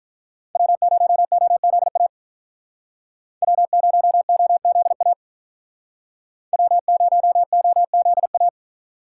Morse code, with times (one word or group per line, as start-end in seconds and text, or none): 0.6-2.1 W0OZA
3.4-5.1 W0OZA
6.5-8.5 W0OZA